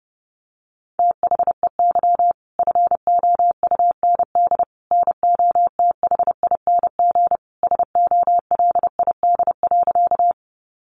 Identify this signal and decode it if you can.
Morse code, heard as a keyed tone, about 30 wpm, 700 Hz. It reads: THEY FOUND NOTHING SOLID.
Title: Sentences from Top 1500 Words 30wpm